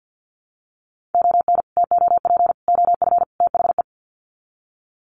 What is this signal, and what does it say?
Morse code, heard as a keyed tone, about 50 wpm, 700 Hz. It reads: ON TOP OF THE
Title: Sets of 4 Words 50wpm